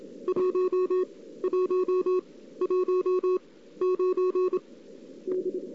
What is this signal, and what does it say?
Morse code, heard as a keyed tone, about 27 wpm, 370 Hz. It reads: 1 1 1 9 D
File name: CW / 2008-ss-cw-k5tr-op-n5rz -02:00